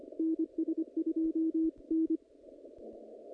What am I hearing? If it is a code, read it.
Morse code, heard as a keyed tone, about 25 wpm, 320 Hz. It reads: NS2N
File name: contests / naqp-cw-k5tr-op-k5pi- -2300